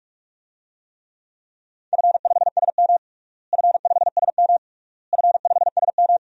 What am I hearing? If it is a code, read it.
Morse code, heard as a keyed tone, about 46 wpm, 700 Hz. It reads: F5SM F5SM F5SM
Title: DXCC-100 Call Signs - 4 Characters - Repeated 3x - 45wpm